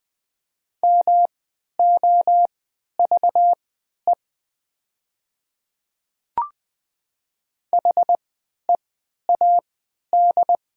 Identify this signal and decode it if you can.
Morse code, heard as a keyed tone, about 20 wpm, spaced as for 11 wpm, 700 Hz. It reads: MOVE HEAD
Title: 3x Character Spacing - Top 300 Words - 20wpm